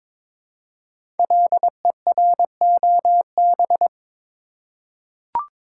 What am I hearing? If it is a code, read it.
Morse code, heard as a keyed tone, about 22 wpm, 700 Hz. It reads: LEROB
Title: Code Groups - 5 Letters - 22wpm